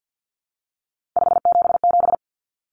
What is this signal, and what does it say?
Morse code, heard as a keyed tone, about 50 wpm, 700 Hz. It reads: 577
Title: Common Cut Numbers - 3-Digit 50wpm